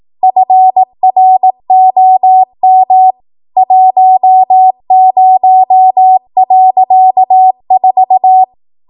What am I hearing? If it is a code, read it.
Morse code, heard as a keyed tone, about 18 wpm, 750 Hz. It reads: FROM 10.4
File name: Archive / 18 WPM / 18WPM